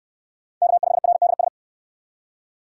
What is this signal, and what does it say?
Morse code, heard as a keyed tone, about 68 wpm, 700 Hz. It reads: K5RDS